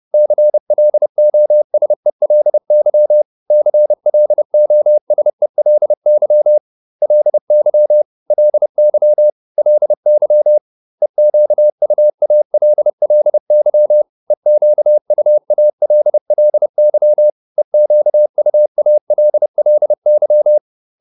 Morse code, heard as a keyed tone, about 30 wpm, 600 Hz. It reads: CLOSELY CLOSELY LY LY LY EQUALLY EQUALLY EQUALLY